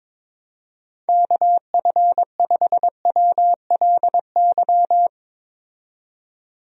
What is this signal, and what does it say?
Morse code, heard as a keyed tone, about 21 wpm, 700 Hz. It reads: KF5WLY
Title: USA Call Signs - 6 Characters - 22wpm